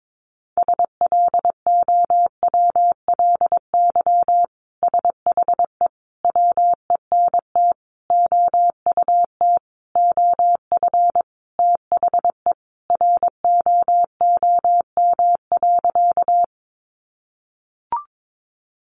Morse code, heard as a keyed tone, about 22 wpm, 700 Hz. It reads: SLOWLY SHE WENT OUT OF THE ROOM.
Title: Sentences from Top 1100 Words 22wpm